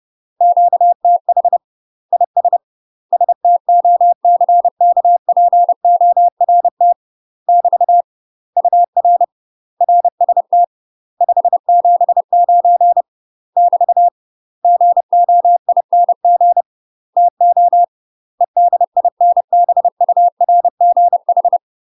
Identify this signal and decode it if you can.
Morse code, heard as a keyed tone, about 30 wpm, 700 Hz. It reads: QTH IS STOCKPORT <BT> UR RST 579 <BT> GOING TO EDINBURGH
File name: UK MASTER QSOs TRAINING / 30WPM / UK MASTER QSOs 0009